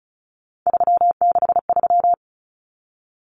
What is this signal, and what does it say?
Morse code, heard as a keyed tone, about 34 wpm, 700 Hz. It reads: 363
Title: Common Cut Numbers - 3-Digit 35wpm